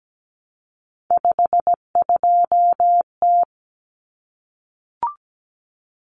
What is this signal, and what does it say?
Morse code, heard as a keyed tone, about 17 wpm, 700 Hz. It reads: 52T
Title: Common Cut Numbers - 3-Digit 17wpm